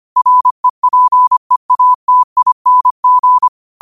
Morse code, heard as a keyed tone, about 25 wpm, 1000 Hz. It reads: REPEATING
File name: qotdfiles / 25wpm